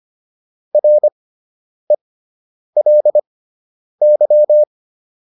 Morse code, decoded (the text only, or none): RELY